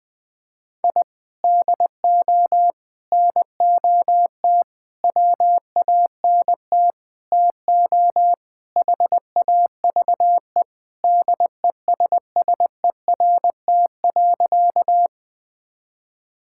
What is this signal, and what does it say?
Morse code, heard as a keyed tone, about 21 wpm, 700 Hz. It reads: I DO NOT WANT TO HAVE DESSERT.
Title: Sentences with 5-Word Phrases - 20wpm